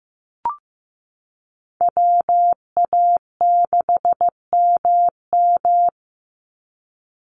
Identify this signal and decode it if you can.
Morse code, heard as a keyed tone, about 14 wpm, 700 Hz. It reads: WA6MM
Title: NA SOTA Chaser Callsigns 15wpm